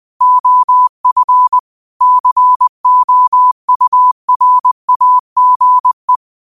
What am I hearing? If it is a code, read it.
Morse code, heard as a keyed tone, about 20 wpm, 1000 Hz. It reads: OF COURAGE
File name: 20wpm